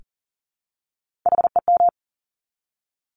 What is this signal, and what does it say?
Morse code, heard as a keyed tone, about 40 wpm, 700 Hz. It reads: HEM
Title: Top Three Letter Combinations in Words 40wpm